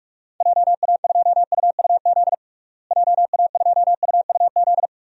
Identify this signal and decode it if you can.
Morse code, heard as a keyed tone, about 46 wpm, 700 Hz. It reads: JA2UUZ JA2UUZ
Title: DXCC-100 Call Signs - 6 Characters - Repeated 3x - 45wpm